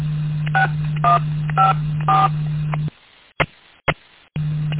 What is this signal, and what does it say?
DTMF, heard as telephone touch tones, tones of about 150 ms. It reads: 6154